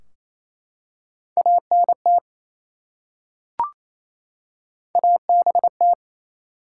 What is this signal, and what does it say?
Morse code, heard as a keyed tone, about 28 wpm, 700 Hz. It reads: ANT ABT
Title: Top 90 QSO Elements 28wpm